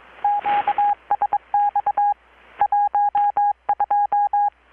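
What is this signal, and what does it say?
Morse code, heard as a keyed tone, about 22 wpm, 820 Hz. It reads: QSX 12
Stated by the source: Morse from KPH's Night of Nights XX maritime radiotelegraphy event, received in Chiapas, Mexico about 3,800 km away on a Kenwood R-600 with a grounded 10 m long-wire antenna.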